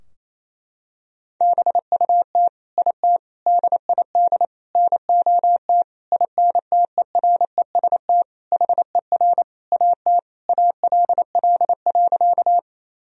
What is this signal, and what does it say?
Morse code, heard as a keyed tone, about 27 wpm, 700 Hz. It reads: BUT IT DID NOT INTEREST HER AT ALL.
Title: Sentences from Top 1000 Words - Review Part 2 - 28wpm